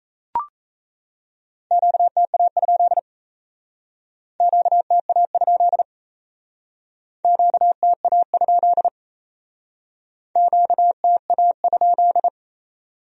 Morse code, heard as a keyed tone, about 37 wpm, 700 Hz. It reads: QTA? QTA? QTA? QTA?